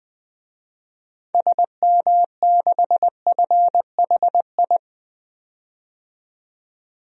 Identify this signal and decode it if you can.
Morse code, heard as a keyed tone, about 20 wpm, 700 Hz. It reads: SM6FHI
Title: DXCC-100 Call Signs - 6 Characters - 20wpm